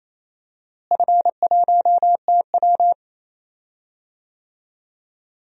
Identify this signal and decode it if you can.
Morse code, heard as a keyed tone, about 28 wpm, 700 Hz. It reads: F1TW
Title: DXCC-100 Call Signs - 4 Characters - Repeated 2x - 28wpm